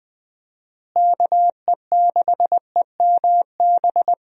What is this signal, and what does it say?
Morse code, heard as a keyed tone, about 21 wpm, 700 Hz. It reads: KE6EMB
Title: USA Call Signs - 6 Characters - Repeated 2x - 20wpm